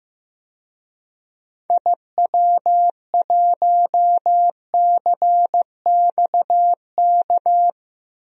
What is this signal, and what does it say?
Morse code, heard as a keyed tone, about 16 wpm, 700 Hz. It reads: IW1CXK
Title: DXCC-100 Call Signs - 6 Characters - Repeated 2x - 15wpm